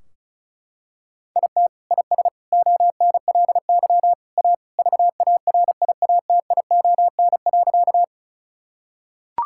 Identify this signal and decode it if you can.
Morse code, heard as a keyed tone, about 36 wpm, 700 Hz. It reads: IT IS ONLY A VARIATION.
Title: Sentences from Top 1600 Words 35wpm